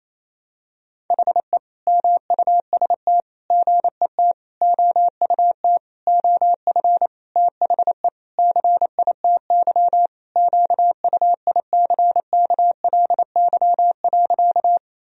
Morse code, heard as a keyed tone, about 28 wpm, 700 Hz. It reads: HE MUST GET OUT OF THE CITY QUICKLY.
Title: Sentences from Top 1000 Words - Review Part 1 - 28wpm